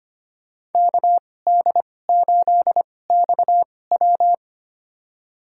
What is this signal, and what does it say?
Morse code, heard as a keyed tone, about 25 wpm, spaced as for 18 wpm, 700 Hz. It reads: KD8XW
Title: USA Call Signs - 5 Characters - 2x Character Spacing - 25wpm